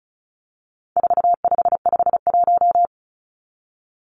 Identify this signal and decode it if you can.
Morse code, heard as a keyed tone, about 34 wpm, 700 Hz. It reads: 4551
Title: Numbers - 4-Digit 35wpm